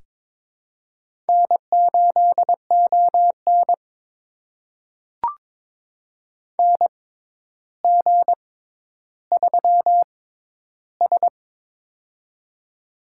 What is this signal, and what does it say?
Morse code, heard as a keyed tone, about 22 wpm, spaced as for 11 wpm, 700 Hz. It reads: N8ON NG3S
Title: USA Call Signs - 6x Character Spacing - 22wpm